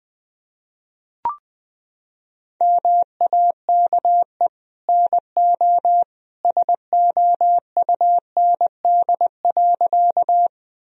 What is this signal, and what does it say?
Morse code, heard as a keyed tone, about 19 wpm, 700 Hz. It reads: MAKE NO SOUND.
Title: Sentences from Top 1000 Words - Review Part 1 - 20wpm